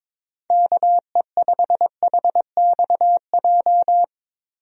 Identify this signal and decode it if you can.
Morse code, heard as a keyed tone, about 22 wpm, 700 Hz. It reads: KE5HXJ
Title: USA Call Signs - 6 Characters - Repeated 2x - 22wpm